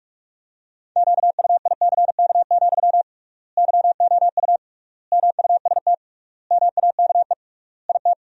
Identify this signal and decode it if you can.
Morse code, heard as a keyed tone, about 41 wpm, 700 Hz. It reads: QUICK, YOU MUST MAKE IT
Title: 3x Word-Spacing - Sentences from Top 500 Words - 45wpm